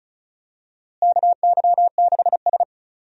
Morse code, heard as a keyed tone, about 36 wpm, 700 Hz. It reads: KY6S
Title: USA Call Signs - 4 Characters - Repeated 3x - 35wpm